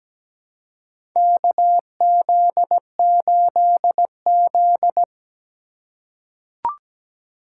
Morse code, heard as a keyed tone, about 17 wpm, 700 Hz. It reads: KZ8Z
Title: USA Call Signs - 6x Character Spacing - 17wpm